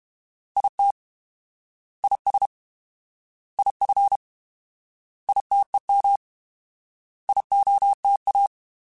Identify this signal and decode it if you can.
Morse code, heard as a keyed tone, about 31 wpm, 800 Hz. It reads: IT IS IF ITEM IOTA
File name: Alphabetical / Alphabetical mp3s / Is @30